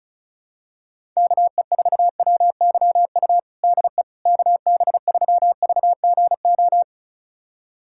Morse code, heard as a keyed tone, about 34 wpm, 700 Hz. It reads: KE4WYU DE KB3VGO
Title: USA Callsign DE Callsign - 6 Characters - 35wpm